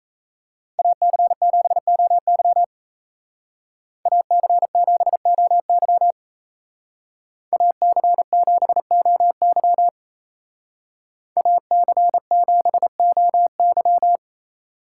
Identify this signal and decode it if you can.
Morse code, heard as a keyed tone, about 36 wpm, 700 Hz. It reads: AC7OY AC7OY AC7OY AC7OY